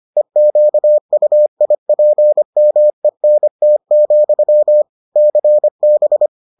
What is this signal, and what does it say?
Morse code, heard as a keyed tone, about 25 wpm, 600 Hz. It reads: EQUIPMENT, CB